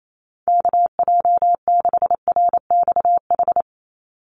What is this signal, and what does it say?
Morse code, heard as a keyed tone, about 28 wpm, 700 Hz. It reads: KJ6RXH
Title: USA Call Signs 28wpm